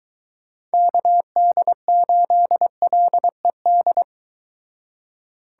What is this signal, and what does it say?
Morse code, heard as a keyed tone, about 23 wpm, 700 Hz. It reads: KD8LED